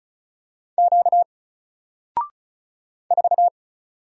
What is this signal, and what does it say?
Morse code, heard as a keyed tone, about 37 wpm, 700 Hz. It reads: Q 4
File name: Single-Letter-Number-35wpm